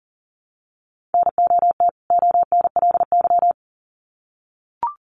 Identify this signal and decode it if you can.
Morse code, heard as a keyed tone, about 40 wpm, 700 Hz. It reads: NOT ONLY
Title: Sets of 2 Words 40wpm